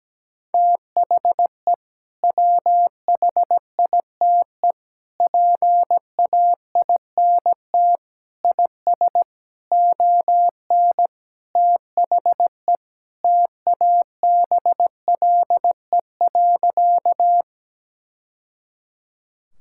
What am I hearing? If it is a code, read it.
Morse code, heard as a keyed tone, about 17 wpm, 700 Hz. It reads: THE WHITE PAINT IS ON THE TABLE.
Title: Sentences from Top 1000 Words - Review Part 6 - 17wpm